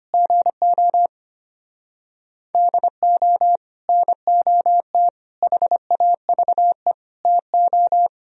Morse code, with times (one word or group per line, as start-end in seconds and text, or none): 0.1-1.1 GO
2.5-3.6 DO
3.9-5.1 NOT
5.4-6.9 HAVE
7.3-8.1 TO